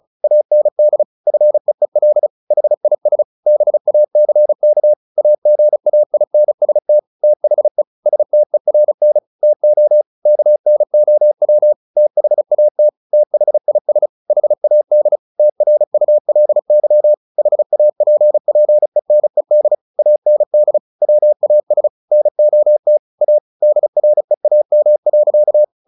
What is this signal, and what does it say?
Morse code, heard as a keyed tone, about 34 wpm, 600 Hz. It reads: AND FEEL HIS BACK AGAINST THE STERN TO KNOW THAT THIS HAD TRULY HAPPENED AND WAS NOT A DREAM.
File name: the old man and the sea 2 35wpm